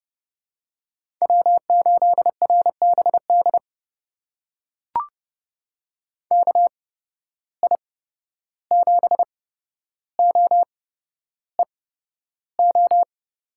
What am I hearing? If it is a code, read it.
Morse code, heard as a keyed tone, about 30 wpm, spaced as for 12 wpm, 700 Hz. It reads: W8RBD KI7OEO
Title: USA Call Signs - 8x Character Spacing - 30wpm